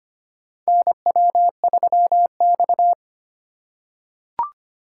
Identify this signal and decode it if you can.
Morse code, heard as a keyed tone, about 25 wpm, 700 Hz. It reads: NW3X
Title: USA Call Signs - 4 Characters - 4x Character Spacing - 25wpm